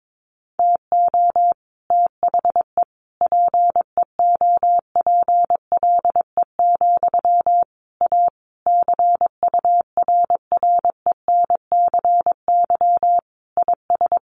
Morse code, heard as a keyed tone, about 22 wpm, 700 Hz. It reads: TO THE PEOPLE, A CURRENCY IS
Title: Sentences from Top 1500 Words 22wpm